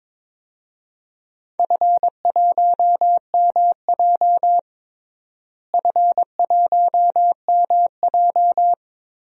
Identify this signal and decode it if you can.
Morse code, heard as a keyed tone, about 22 wpm, 700 Hz. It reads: F1MJ F1MJ
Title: DXCC-100 Call Signs - 4 Characters - Repeated 3x - 22wpm